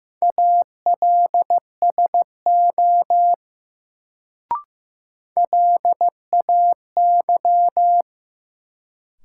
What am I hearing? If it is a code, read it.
Morse code, heard as a keyed tone, about 15 wpm, 700 Hz. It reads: ALSO LAY
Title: Sentences from Top 400 Words - New Word Drill - 15wpm